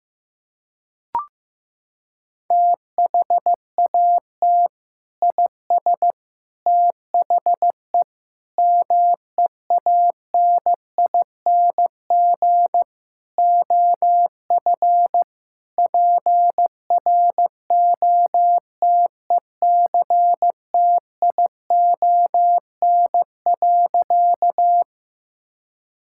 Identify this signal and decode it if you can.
Morse code, heard as a keyed tone, about 15 wpm, 700 Hz. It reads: THAT IS THE MEANING OF PROTECTION.
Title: Sentences from Top 1600 Words 15wpm